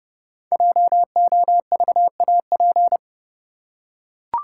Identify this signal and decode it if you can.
Morse code, heard as a keyed tone, about 29 wpm, 700 Hz. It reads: JOVAP
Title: Code Groups - 5 Letters - 30wpm